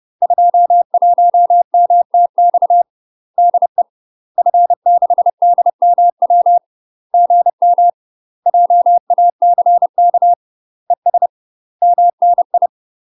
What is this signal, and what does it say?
Morse code, heard as a keyed tone, about 30 wpm, 700 Hz. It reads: 21MTX DE F6DMW GM JACK ES MNI
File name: UK MASTER QSOs TRAINING / 30WPM / UK MASTER QSOs 0011